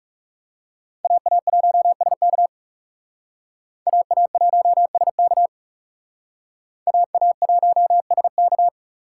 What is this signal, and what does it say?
Morse code, heard as a keyed tone, about 37 wpm, 700 Hz. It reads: AA1SK AA1SK AA1SK
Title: Speed Racing - USA Call Signs 30wpm